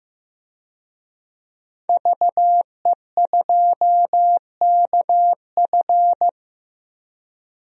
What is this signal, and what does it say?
Morse code, heard as a keyed tone, about 15 wpm, 700 Hz. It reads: VE2KF
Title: DXCC-100 Call Signs - 5 Characters - 15wpm